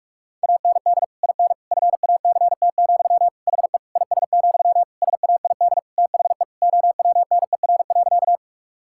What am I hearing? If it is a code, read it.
Morse code, heard as a keyed tone, about 46 wpm, 700 Hz. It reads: AND IN FACT, HE IS, SAID THE OWNER.
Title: Sentences from Top 1600 Words 45wpm